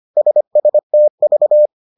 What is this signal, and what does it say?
Morse code, heard as a keyed tone, about 25 wpm, 600 Hz. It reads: SSTV